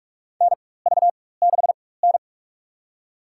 Morse code, heard as a keyed tone, about 42 wpm, spaced as for 21 wpm, 700 Hz. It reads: NV6N
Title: USA Call Signs - 4 Characters - 4x Character Spacing - 45wpm